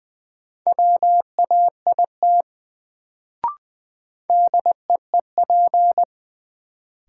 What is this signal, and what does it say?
Morse code, heard as a keyed tone, about 20 wpm, 700 Hz. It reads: WAIT DEEP
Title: Sentences from Top 500 Words - New Word Drill - 20wpm